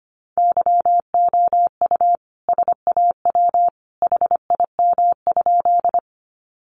Morse code, heard as a keyed tone, about 25 wpm, 700 Hz. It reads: YOU SAW HIM?
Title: Sentences from Top 300 Words 25wpm